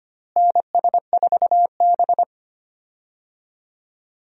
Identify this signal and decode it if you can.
Morse code, heard as a keyed tone, about 25 wpm, 700 Hz. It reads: NS4B
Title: USA Call Signs - 4 Characters - Repeated 3x - 25wpm